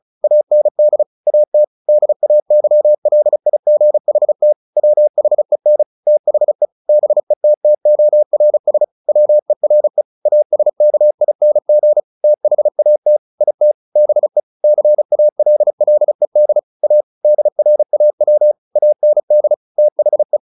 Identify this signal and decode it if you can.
Morse code, heard as a keyed tone, about 34 wpm, 600 Hz. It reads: AND AT DAYLIGHT WHEN THE BETTORS WERE ASKING THAT IT BE CALLED A DRAW AND THE
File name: the old man and the sea 2 35wpm